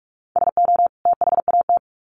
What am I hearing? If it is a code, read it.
Morse code, heard as a keyed tone, about 44 wpm, 700 Hz. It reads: SO THAT
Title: Sets of 2 Words 45wpm